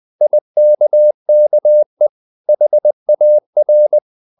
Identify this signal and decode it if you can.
Morse code, heard as a keyed tone, about 20 wpm, 600 Hz. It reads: IKKE HAR
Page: Ferie 20 wpm